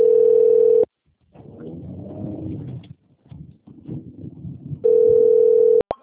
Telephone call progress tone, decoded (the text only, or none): ringback tone